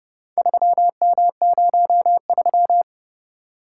Morse code, heard as a keyed tone, about 28 wpm, 700 Hz. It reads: 3M03